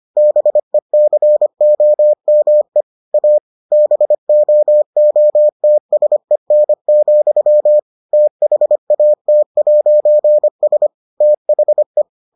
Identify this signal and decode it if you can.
Morse code, heard as a keyed tone, about 25 wpm, 600 Hz. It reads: BECOME A BOOTSEN, THAT'S THE